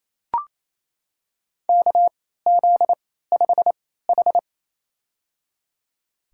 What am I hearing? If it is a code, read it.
Morse code, heard as a keyed tone, about 28 wpm, spaced as for 16 wpm, 700 Hz. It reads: KZ5H
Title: USA Call Signs - 4 Characters - 3x Character Spacing - 28wpm